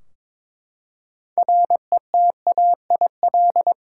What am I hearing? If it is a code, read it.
Morse code, heard as a keyed tone, about 22 wpm, 700 Hz. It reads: RETAIL
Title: Practice QSO - Exchange 3 - More Info - 22wpm